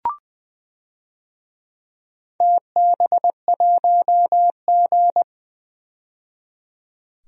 Morse code, heard as a keyed tone, about 20 wpm, 700 Hz. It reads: TB1G
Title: DXCC-100 Call Signs - 4 Characters - 20wpm